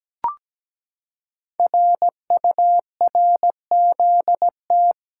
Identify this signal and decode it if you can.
Morse code, heard as a keyed tone, about 17 wpm, 700 Hz. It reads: RURZT
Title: Code Groups - 5 Letters - 17wpm